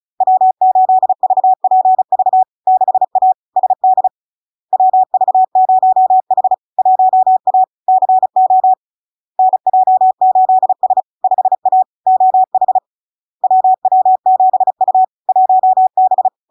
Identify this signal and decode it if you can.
Morse code, heard as a keyed tone, about 36 wpm, 750 Hz. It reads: W8VPV 6A SD WV0H 1A CO NJ8S 5A OH WW7U 1B
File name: LARK-FD-2022-Log-in-CW-35wpm-750-Hz